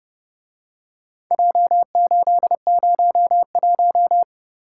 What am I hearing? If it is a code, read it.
Morse code, heard as a keyed tone, about 31 wpm, 700 Hz. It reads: J801